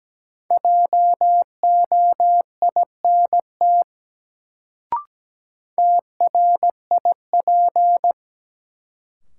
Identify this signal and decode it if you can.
Morse code, heard as a keyed tone, about 17 wpm, 700 Hz. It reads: JOINT TRIP
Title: Sentences from Top 1300 Words - New Word Drill Alt - 17wpm